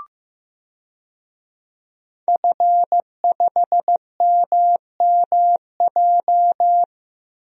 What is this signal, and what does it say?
Morse code, heard as a keyed tone, about 15 wpm, 700 Hz. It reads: F5MMJ
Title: DXCC-100 Call Signs - 5 Characters - 15wpm